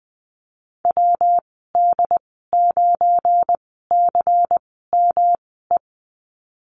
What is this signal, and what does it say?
Morse code, heard as a keyed tone, about 20 wpm, spaced as for 15 wpm, 700 Hz. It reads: WD9CME
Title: 186 Lesson 30g - Letter b - Call Signs 20wpm